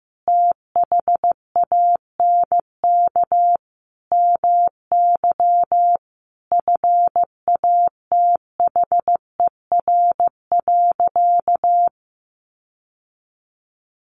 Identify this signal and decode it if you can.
Morse code, heard as a keyed tone, about 15 wpm, 700 Hz. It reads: THANK MY FATHER.
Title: Sentences from Top 1100 Words 15wpm